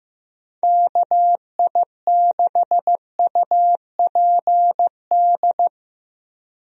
Morse code, heard as a keyed tone, about 15 wpm, 700 Hz. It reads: KI6UPD
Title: USA Call Signs - 6 Characters - Repeated 3x - 15wpm